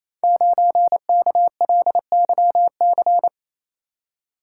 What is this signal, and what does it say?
Morse code, heard as a keyed tone, about 29 wpm, 700 Hz. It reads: 9KLYC
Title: Code Groups - 5 Letter:Numbers - 28wpm